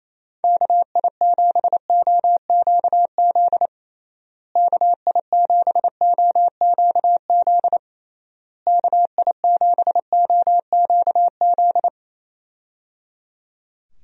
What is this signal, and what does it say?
Morse code, heard as a keyed tone, about 28 wpm, 700 Hz. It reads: KI7OQZ KI7OQZ KI7OQZ
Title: USA Call Signs - 6 Characters - Repeated 3x - 28wpm